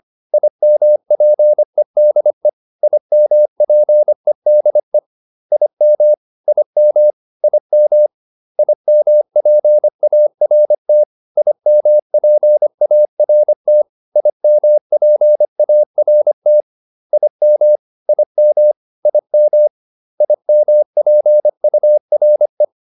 Morse code, heard as a keyed tone, about 25 wpm, 600 Hz. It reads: IMPEDE IMPEDE IM IM IM IMPART IMPART IMPART IM IM IM IMPURE